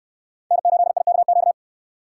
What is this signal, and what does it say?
Morse code, heard as a keyed tone, about 64 wpm, 700 Hz. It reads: N8IKQ